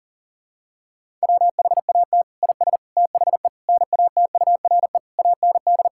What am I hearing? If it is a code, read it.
Morse code, heard as a keyed tone, about 40 wpm, 700 Hz. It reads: WHAT IS THE NATURE AND
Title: Sentences from Top 1000 Words - Review Part 5 - 40wpm